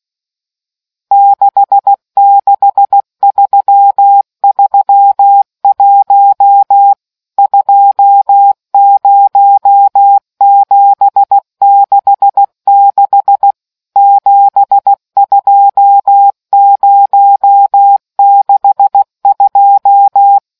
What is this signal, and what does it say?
Morse code, heard as a keyed tone, about 16 wpm, 800 Hz. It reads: 66331 20766 72062